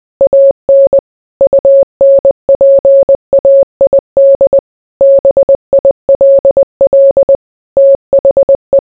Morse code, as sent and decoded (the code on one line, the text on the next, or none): .- -. / ..- -. .--. .- .. -.. / -... .. .-.. .-.. / - .... .
AN UNPAID BILL THE